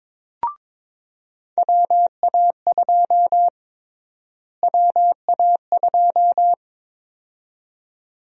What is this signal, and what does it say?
Morse code, heard as a keyed tone, about 22 wpm, 700 Hz. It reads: WA2 WA2
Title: USA Call Sign Prefixes - Repeated 2x - 22wpm